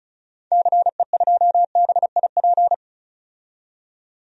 Morse code, heard as a keyed tone, about 36 wpm, 700 Hz. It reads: CE2BIP
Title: DXCC-100 Call Signs - 6 Characters - 35wpm